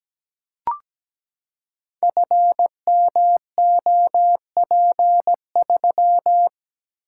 Morse code, heard as a keyed tone, about 17 wpm, 700 Hz. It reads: FMOP3
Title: Code Groups - 5 Letter:Numbers - 17wpm